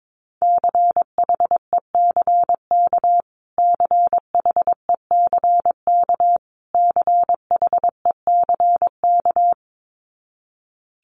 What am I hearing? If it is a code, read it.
Morse code, heard as a keyed tone, about 22 wpm, 700 Hz. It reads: CHECK CHECK CHECK
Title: Top 500 Words Repeated 22wpm